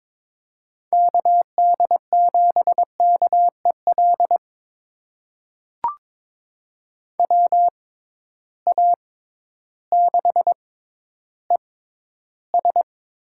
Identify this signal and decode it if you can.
Morse code, heard as a keyed tone, about 22 wpm, spaced as for 12 wpm, 700 Hz. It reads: KD7KEL WA6ES
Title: USA Call Signs - 6x Character Spacing - 22wpm